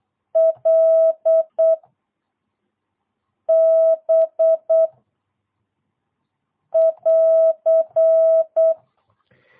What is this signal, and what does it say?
Morse code, heard as a keyed tone, about 8 wpm, 650 Hz. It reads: L B <AR>
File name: jh8cbh / 2022cw